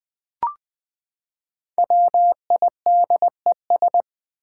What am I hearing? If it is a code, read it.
Morse code, heard as a keyed tone, about 20 wpm, 700 Hz. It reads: WIDES
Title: Build A Word - 5 Letters At A Time - 10-Letter Words - 20wpm